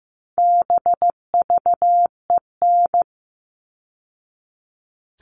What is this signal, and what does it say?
Morse code, heard as a keyed tone, about 15 wpm, 700 Hz. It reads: BVEN
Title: All Cut Numbers - 4-Digit 15wpm